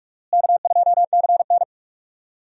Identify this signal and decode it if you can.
Morse code, heard as a keyed tone, about 42 wpm, 700 Hz. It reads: K2CN
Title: USA Call Signs - 4 Characters - Repeated 2x - 45wpm